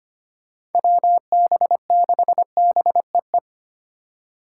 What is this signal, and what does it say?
Morse code, heard as a keyed tone, about 25 wpm, 700 Hz. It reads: WB6BEE
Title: K1USN SST QSO Practice - 25wpm